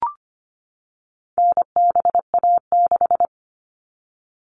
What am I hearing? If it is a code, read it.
Morse code, heard as a keyed tone, about 25 wpm, 700 Hz. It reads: NBA6
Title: All Cut Numbers - 4-Digit 25wpm